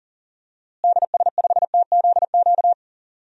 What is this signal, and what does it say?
Morse code, heard as a keyed tone, about 40 wpm, 700 Hz. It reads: DS5TZQ
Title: DXCC-100 Call Signs - 6 Characters - Repeated 3x - 40wpm